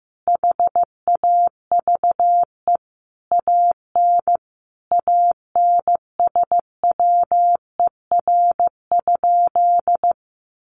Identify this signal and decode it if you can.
Morse code, heard as a keyed tone, about 15 wpm, 700 Hz. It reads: HAVE AN ANSWER?
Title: Sentences from Top 1300 Words 15wpm